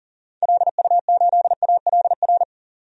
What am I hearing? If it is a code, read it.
Morse code, heard as a keyed tone, about 40 wpm, 700 Hz. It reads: LU8ALR
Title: DXCC-100 Call Signs - 6 Characters - 40wpm